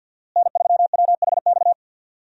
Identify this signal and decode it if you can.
Morse code, heard as a keyed tone, about 48 wpm, 700 Hz. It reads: N3WHX
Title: USA Call Signs - 5 Characters - Repeated 3x - 50wpm